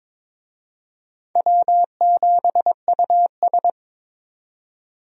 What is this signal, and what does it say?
Morse code, heard as a keyed tone, about 22 wpm, 700 Hz. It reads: W7US
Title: USA Call Signs - 4 Characters - 22wpm